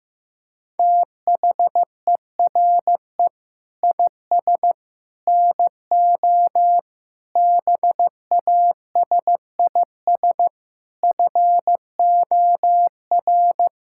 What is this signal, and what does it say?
Morse code, heard as a keyed tone, about 15 wpm, 700 Hz. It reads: THERE IS NO BASIS FOR
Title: Sentences from Top 1000 Words - Review Part 5 - 15wpm